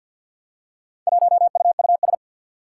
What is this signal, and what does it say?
Morse code, heard as a keyed tone, about 52 wpm, 700 Hz. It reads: 1UUS
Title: Four Challenging Letter-Number Combinations 50wpm